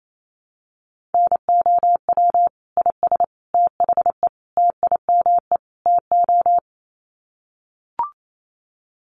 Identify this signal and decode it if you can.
Morse code, heard as a keyed tone, about 28 wpm, 700 Hz. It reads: NOW IS THE TIME TO
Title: Sets of 5 Words Encore2 28wpm